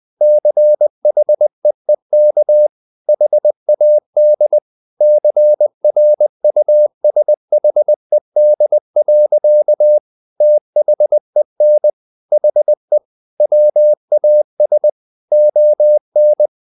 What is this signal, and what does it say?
Morse code, heard as a keyed tone, about 20 wpm, 600 Hz. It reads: CHEEK HAD CRUSHED. THEN HE WAS ON
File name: the old man and the sea part 2 20wpm